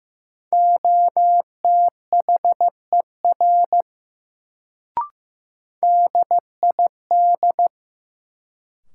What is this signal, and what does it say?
Morse code, heard as a keyed tone, about 15 wpm, 700 Hz. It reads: OTHER DID
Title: Sentences from Top 100 Words - New Word Drill - 15wpm